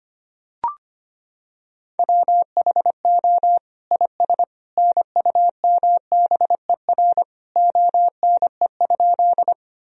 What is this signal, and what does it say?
Morse code, heard as a keyed tone, about 25 wpm, 700 Hz. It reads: WHO IS NUMBER ONE?
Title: Sentences from Top 1000 Words - Review Part 1 - 25wpm